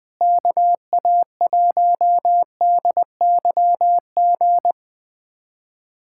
Morse code, heard as a keyed tone, about 20 wpm, 700 Hz. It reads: KA1DYG
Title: USA Callsign DE Callsign - 6 Characters - 20wpm